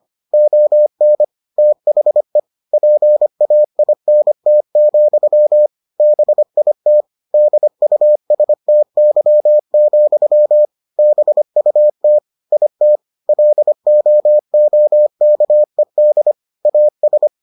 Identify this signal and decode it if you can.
Morse code, heard as a keyed tone, about 25 wpm, 600 Hz. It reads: ON THE PAINT, BIT DUSTY, BUT IT LOOKED AS